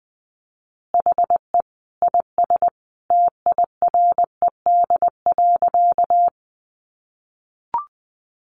Morse code, heard as a keyed tone, about 19 wpm, 700 Hz. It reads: HE IS TIRED.
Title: Additional Easy Phrases 20wpm